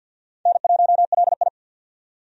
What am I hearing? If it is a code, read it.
Morse code, heard as a keyed tone, about 52 wpm, 700 Hz. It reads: N1LI
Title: USA Call Signs - 4 Characters - 50wpm